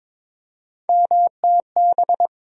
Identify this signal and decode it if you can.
Morse code, heard as a keyed tone, about 22 wpm, 700 Hz. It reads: MTB
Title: USA Call Sign Postfixes - 3 Letters 22wpm